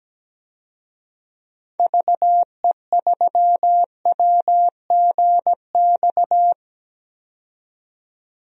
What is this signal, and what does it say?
Morse code, heard as a keyed tone, about 17 wpm, 700 Hz. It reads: VE3WGX
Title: DXCC-100 Call Signs - 6 Characters - 17wpm